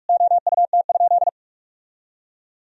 Morse code, heard as a keyed tone, about 41 wpm, 700 Hz. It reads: OUT?
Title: Sentences from Top 1000 Words - Review Part 4 - 45wpm